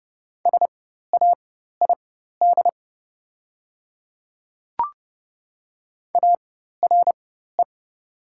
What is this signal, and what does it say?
Morse code, heard as a keyed tone, about 30 wpm, spaced as for 14 wpm, 700 Hz. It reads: SAID ARE
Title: 4x Character Spacing - Top 200 Words - 30wpm